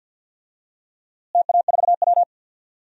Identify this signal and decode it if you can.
Morse code, heard as a keyed tone, about 51 wpm, 700 Hz. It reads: TA4W
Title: DXCC-100 Call Signs - 4 Characters - Repeated 3x - 50wpm